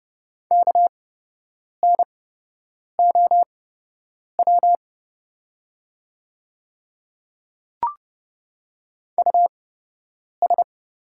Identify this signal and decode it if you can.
Morse code, heard as a keyed tone, about 30 wpm, spaced as for 8 wpm, 700 Hz. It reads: KNOW US
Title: 8x Character Spacing - Top 200 Words - 30wpm